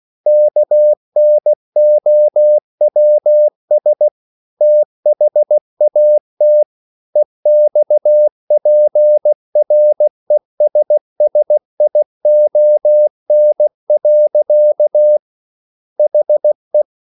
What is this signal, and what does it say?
Morse code, heard as a keyed tone, about 16 wpm, 600 Hz. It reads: KNOWS THAT EXPRESSION. HE